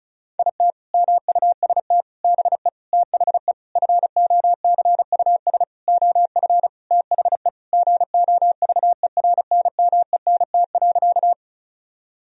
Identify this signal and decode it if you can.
Morse code, heard as a keyed tone, about 36 wpm, 700 Hz. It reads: IT MUST BE THE FOCUS OF THE GOVERNMENT.
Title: Sentences from Top 1000 Words - Review Part 4 - 35wpm